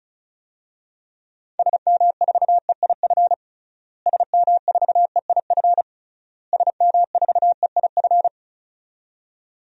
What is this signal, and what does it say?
Morse code, heard as a keyed tone, about 36 wpm, 700 Hz. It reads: SM4EIF SM4EIF SM4EIF
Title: DXCC-100 Call Signs - 6 Characters - Repeated 3x - 35wpm